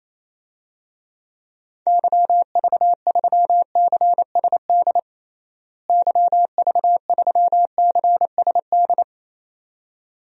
Morse code, heard as a keyed tone, about 28 wpm, 700 Hz. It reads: YV3CSD YV3CSD
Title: DXCC-100 Call Signs - 6 Characters - Repeated 2x - 28wpm